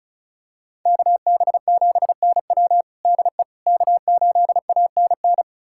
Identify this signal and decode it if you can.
Morse code, heard as a keyed tone, about 36 wpm, 700 Hz. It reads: KB7NW DE K8ANN
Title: USA Callsign DE Callsign - 5 Characters - 35wpm